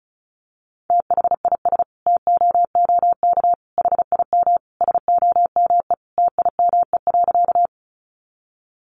Morse code, heard as a keyed tone, about 34 wpm, 700 Hz. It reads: THIS TOOK HIM SOME TIME.
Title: Sentences from Top 300 Words 35wpm